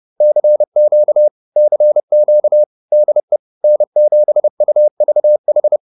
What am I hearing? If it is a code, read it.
Morse code, heard as a keyed tone, about 30 wpm, 600 Hz. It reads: CQ CQ DE N7UVH